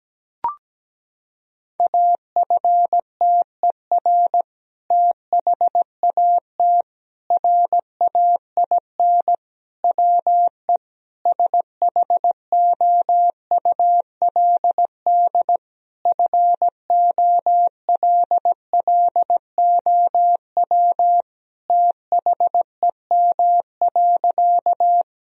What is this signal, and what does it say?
Morse code, heard as a keyed tone, about 17 wpm, 700 Hz. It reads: AFTER THAT RAIN WE SHOULD FOLLOW THEM.
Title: Sentences from Top 1000 Words - Review Part 1 - 17wpm